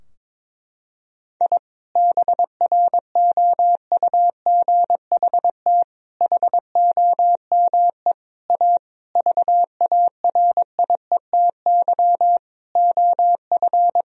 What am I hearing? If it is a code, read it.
Morse code, heard as a keyed tone, about 22 wpm, 700 Hz. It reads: I BROUGHT HOME A VARIETY OF
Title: Sentences from Top 1000 Words - Review Part 4 - 22wpm